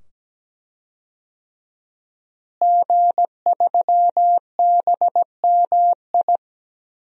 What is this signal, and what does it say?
Morse code, heard as a keyed tone, about 17 wpm, 700 Hz. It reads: G3BMI
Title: DXCC-100 Call Signs - 5 Characters - Repeated 2x - 17wpm